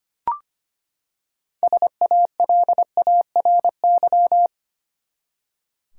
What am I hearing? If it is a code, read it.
Morse code, heard as a keyed tone, about 25 wpm, 700 Hz. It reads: SALARY
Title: Six Letter Words - 25wpm